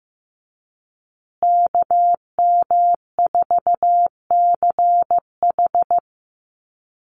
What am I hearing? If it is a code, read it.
Morse code, heard as a keyed tone, about 15 wpm, 700 Hz. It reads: KM4CH
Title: Call Signs Repeated 2x - 15wpm